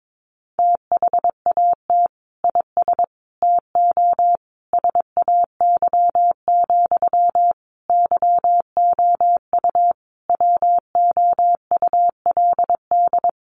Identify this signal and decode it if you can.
Morse code, heard as a keyed tone, about 22 wpm, 700 Hz. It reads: THAT IS TO SAY, YOU WOULD
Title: Sentences from Top 700 Words 22wpm